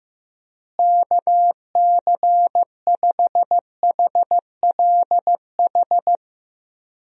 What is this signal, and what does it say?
Morse code, heard as a keyed tone, about 15 wpm, 700 Hz. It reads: KC5HLH
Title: USA Call Signs - 6 Characters - Repeated 2x - 15wpm